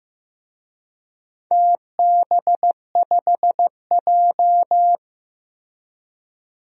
Morse code, heard as a keyed tone, about 15 wpm, 700 Hz. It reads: TB5J
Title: DXCC-100 Call Signs - 4 Characters - Repeated 2x - 15wpm